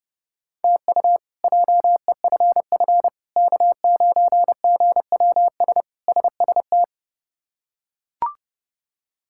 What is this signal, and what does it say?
Morse code, heard as a keyed tone, about 30 wpm, 700 Hz. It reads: TU JEFF K9GWS SST
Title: K1USN SST QSO Practice - 30wpm